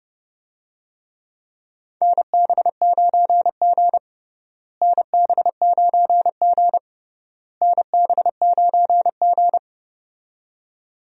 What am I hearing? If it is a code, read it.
Morse code, heard as a keyed tone, about 30 wpm, 700 Hz. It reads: NB9G NB9G NB9G
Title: USA Call Signs - 4 Characters - Repeated 3x - 30wpm